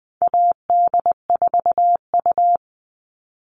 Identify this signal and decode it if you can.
Morse code, heard as a keyed tone, about 20 wpm, 700 Hz. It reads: AD4U
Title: All Cut Numbers - 4-Digit 20wpm